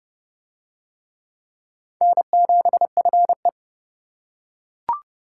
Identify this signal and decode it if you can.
Morse code, heard as a keyed tone, about 25 wpm, 700 Hz. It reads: N7FE
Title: Speed Racing - USA 1x2 Call Signs 20wpm